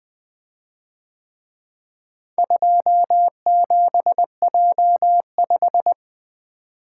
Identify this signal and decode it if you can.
Morse code, heard as a keyed tone, about 21 wpm, 700 Hz. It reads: 27J5
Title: Speed-Racing - Four Letter-Number Combinations - 20wpm